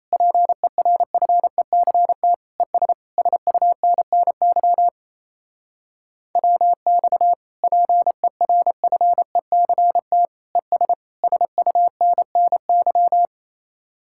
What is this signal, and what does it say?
Morse code, heard as a keyed tone, about 34 wpm, 700 Hz. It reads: PERFECT ES SUNNY WX PERFECT ES SUNNY
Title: Speed-Racing Common QSO Phrases Encore - 28wpm